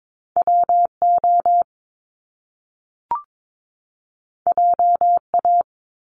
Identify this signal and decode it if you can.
Morse code, heard as a keyed tone, about 22 wpm, 700 Hz. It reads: WO JA
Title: Top Two Letter Combinations in Words 22wpm